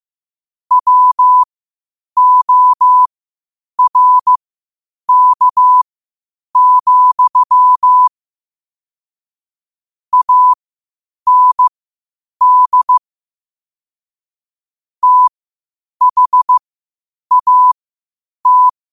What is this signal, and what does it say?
Morse code, heard as a keyed tone, about 15 wpm, spaced as for 8 wpm, 1000 Hz. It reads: WORK, AND THAT
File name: qotdfiles / 8wpm